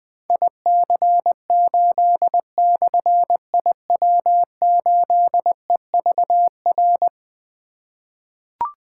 Morse code, heard as a keyed tone, about 21 wpm, 700 Hz. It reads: IC8/IW8EVR
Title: Difficult Call Signs 20wpm